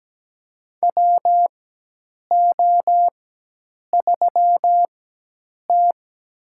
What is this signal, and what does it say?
Morse code, heard as a keyed tone, about 17 wpm, spaced as for 8 wpm, 700 Hz. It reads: WO3T
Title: USA Call Signs - 4 Characters - 4x Character Spacing - 17wpm